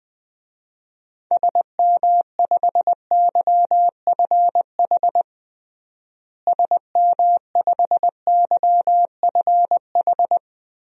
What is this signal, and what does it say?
Morse code, heard as a keyed tone, about 20 wpm, 700 Hz. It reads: SM5YFH SM5YFH
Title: DXCC-100 Call Signs - 6 Characters - Repeated 2x - 20wpm